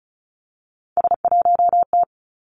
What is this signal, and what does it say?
Morse code, heard as a keyed tone, about 36 wpm, 700 Hz. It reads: S1T
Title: Three Challenging Letter-Number Combinations 35wpm